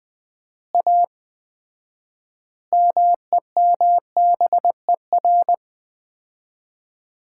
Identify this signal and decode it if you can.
Morse code, heard as a keyed tone, about 19 wpm, 700 Hz. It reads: A MEMBER
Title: 4x Word-Spacing - Sets of 5 Words - 20wpm